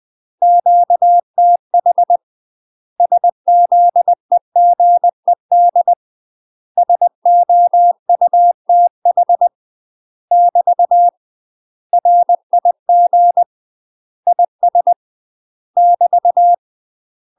Morse code, heard as a keyed tone, about 20 wpm, 700 Hz. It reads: QTH SZEGED SOUTH <BT> RIG IS <BT>
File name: UK MASTER QSOs TRAINING / 20WPM / UK MASTER QSOs 0018